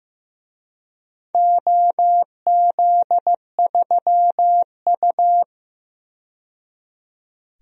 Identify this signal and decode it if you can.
Morse code, heard as a keyed tone, about 16 wpm, 700 Hz. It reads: OZ3U
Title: DXCC-100 Call Signs - 4 Characters - 15wpm